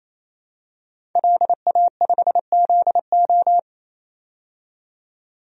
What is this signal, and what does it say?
Morse code, heard as a keyed tone, about 27 wpm, 700 Hz. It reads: LA5ZO
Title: DXCC-100 Call Signs - 5 Characters - 28wpm